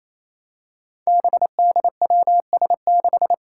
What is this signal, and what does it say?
Morse code, heard as a keyed tone, about 28 wpm, 700 Hz. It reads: BDWS6
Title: Five Challenging Letter-Number Combinations 28wpm